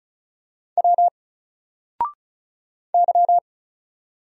Morse code, heard as a keyed tone, about 34 wpm, 700 Hz. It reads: W Y
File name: Single-Letter-Number-35wpm